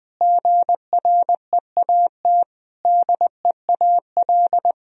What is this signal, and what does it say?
Morse code, heard as a keyed tone, about 20 wpm, 700 Hz. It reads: GREAT DEAL
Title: Sentences from Top 1000 Words - Review Part 1 - 20wpm